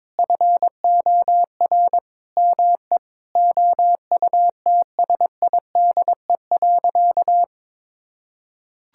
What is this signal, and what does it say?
Morse code, heard as a keyed tone, about 21 wpm, 700 Hz. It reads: FOR ME OUTSIDE.
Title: Sentences from Top 1000 Words - Review Part 6 - 22wpm